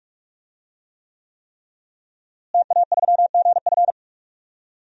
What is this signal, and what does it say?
Morse code, heard as a keyed tone, about 42 wpm, 700 Hz. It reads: TA3GF
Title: DXCC-100 Call Signs - 5 Characters - Repeated 3x - 45wpm